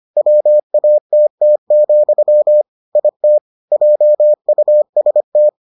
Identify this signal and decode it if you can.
Morse code, heard as a keyed tone, about 25 wpm, 600 Hz. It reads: WATT, IT JUST